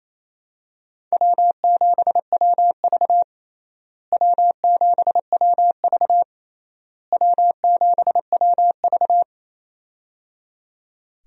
Morse code, heard as a keyed tone, about 28 wpm, 700 Hz. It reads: W7WV W7WV W7WV
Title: USA Call Signs - 4 Characters - Repeated 3x - 28wpm